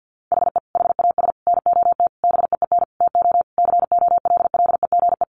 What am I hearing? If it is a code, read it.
Morse code, heard as a keyed tone, about 50 wpm, 700 Hz. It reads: HE HAS NOT BEEN TO COLLEGE
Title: Sentences from Top 1300 Words 50wpm